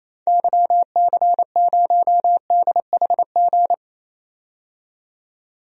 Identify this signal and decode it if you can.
Morse code, heard as a keyed tone, about 28 wpm, 700 Hz. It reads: YC0DHG
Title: DXCC-100 Call Signs - 6 Characters - 28wpm